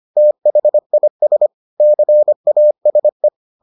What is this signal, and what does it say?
Morse code, heard as a keyed tone, about 25 wpm, 600 Hz. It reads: THIS CASE